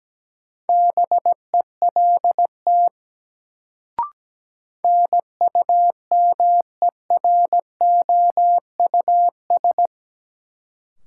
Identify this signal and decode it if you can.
Morse code, heard as a keyed tone, about 17 wpm, 700 Hz. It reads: BELT NUMEROUS
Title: Sentences from Top 1400 Words - New Word Drill Alt - 17wpm